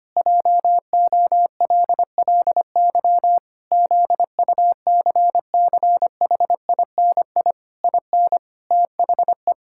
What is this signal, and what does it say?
Morse code, heard as a keyed tone, about 25 wpm, 700 Hz. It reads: JOLLY ZUCCHINI IN THE
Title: Pangram Challenge v2 - 25wpm